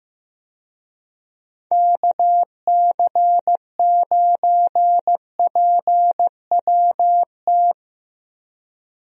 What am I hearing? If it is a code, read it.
Morse code, heard as a keyed tone, about 15 wpm, 700 Hz. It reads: KC9PWT
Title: USA Call Signs - 6 Characters - Repeated 2x - 15wpm